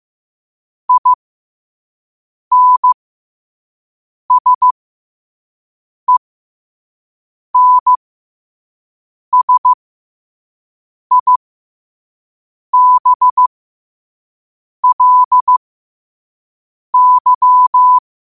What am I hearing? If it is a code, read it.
Morse code, heard as a keyed tone, about 15 wpm, spaced as for 5 wpm, 1000 Hz. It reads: INSENSIBLY
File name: qotdfiles / 5wpm